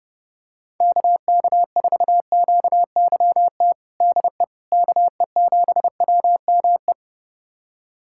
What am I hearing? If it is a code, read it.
Morse code, heard as a keyed tone, about 31 wpm, 700 Hz. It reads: KK4QYT DE KE7WME
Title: USA Callsign DE Callsign - 6 Characters - 30wpm